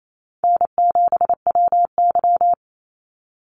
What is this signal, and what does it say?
Morse code, heard as a keyed tone, about 28 wpm, 700 Hz. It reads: N7WY
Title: Easy Call Signs - 28wpm